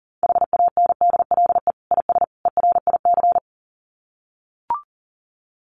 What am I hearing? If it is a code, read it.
Morse code, heard as a keyed tone, about 40 wpm, 700 Hz. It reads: HANDLE IS ERIC
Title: Common QSO Phrases 40wpm